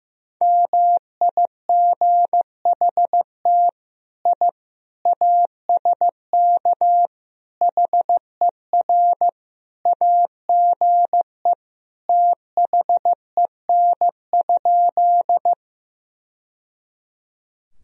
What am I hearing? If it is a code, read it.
Morse code, heard as a keyed tone, about 15 wpm, 700 Hz. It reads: MIGHT I ASK HER AGE THEN?
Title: Sentences from Top 1000 Words - Review Part 3 - 15wpm